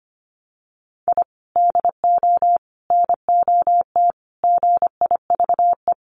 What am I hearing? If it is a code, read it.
Morse code, heard as a keyed tone, about 25 wpm, 700 Hz. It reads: I DO NOT GIVE
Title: Sets of 5 Words Encore2 25wpm